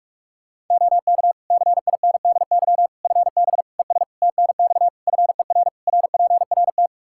Morse code, heard as a keyed tone, about 41 wpm, 700 Hz. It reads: OK CINDY FB ES TNX FER RPRT
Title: Practice QSO - Exchange 2 - Info - 45wpm